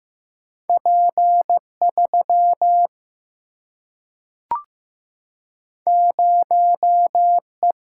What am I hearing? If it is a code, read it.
Morse code, heard as a keyed tone, about 15 wpm, 700 Hz. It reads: P3 0E
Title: Single-Letter Single-Number 15wpm